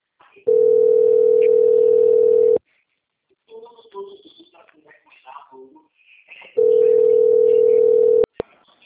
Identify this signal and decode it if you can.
Ringback tone (North America), 440 + 480 Hz, with speech.